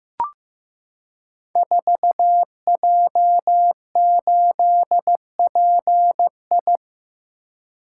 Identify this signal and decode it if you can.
Morse code, heard as a keyed tone, about 15 wpm, 700 Hz. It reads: 4J8PI
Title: Code Groups - 5 Letter:Numbers - 15wpm